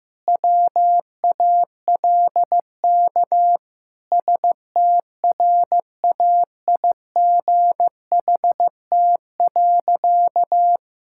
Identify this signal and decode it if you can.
Morse code, heard as a keyed tone, about 15 wpm, 700 Hz. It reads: WALK STRAIGHT.
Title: Sentences from Top 1000 Words - Review Part 6 - 15wpm